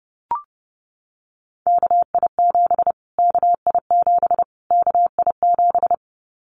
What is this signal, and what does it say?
Morse code, heard as a keyed tone, about 30 wpm, 700 Hz. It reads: KI7 KI7 KI7
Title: Callsign Prefixes for Top 25 DXCC - Repeated 3x - 30wpm